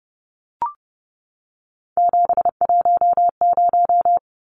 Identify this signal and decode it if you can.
Morse code, heard as a keyed tone, about 30 wpm, 700 Hz. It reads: 710
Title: Numbers - 3-Digit 30wpm